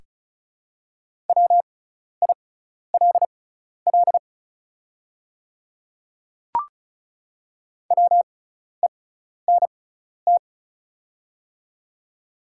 Morse code, heard as a keyed tone, about 34 wpm, spaced as for 12 wpm, 690 Hz. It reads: WILL WENT
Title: 6x Character Spacing - Top 500 Words - 35wpm